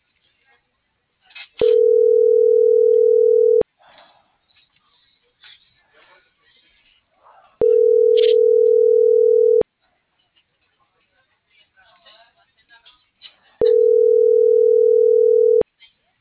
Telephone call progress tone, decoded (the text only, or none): ringback tone